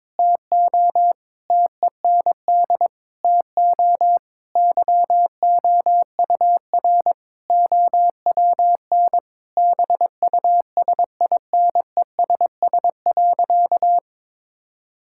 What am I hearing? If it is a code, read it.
Morse code, heard as a keyed tone, about 22 wpm, 700 Hz. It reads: TO TEND TO YOUR OWN BUSINESS.
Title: Sentences from Top 1000 Words - Review Part 5 - 22wpm